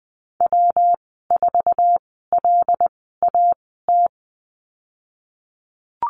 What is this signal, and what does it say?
Morse code, heard as a keyed tone, about 20 wpm, spaced as for 15 wpm, 700 Hz. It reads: W4LAT
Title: 140 Lesson 23g - Letter f - Call Signs 20wpm